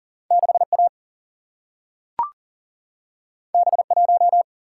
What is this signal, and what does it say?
Morse code, heard as a keyed tone, about 40 wpm, 700 Hz. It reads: 6A B1
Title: Single-Letter Single-Number 40wpm